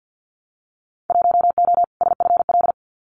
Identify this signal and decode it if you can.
Morse code, heard as a keyed tone, about 50 wpm, 700 Hz. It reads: 1O SFL
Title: Winter Field Day Exchanges 50wpm